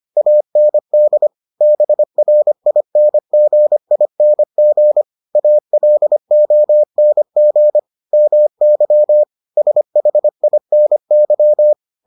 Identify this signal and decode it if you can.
Morse code, heard as a keyed tone, about 25 wpm, 600 Hz. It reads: AND BRINGING ALONG MY SHINY